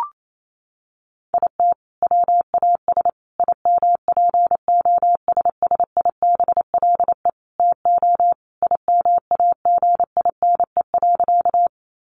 Morse code, heard as a keyed tone, about 28 wpm, 700 Hz. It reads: IT WAS IMPOSSIBLE TO IMAGINE.
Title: Sentences from Top 1400 Words 28wpm